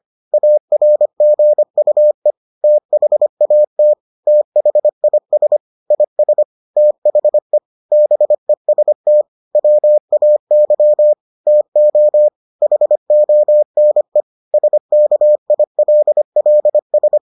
Morse code, heard as a keyed tone, about 25 wpm, 600 Hz. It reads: ARGUE THAT THIS IS THE BEST WAY TO HONE SKILLS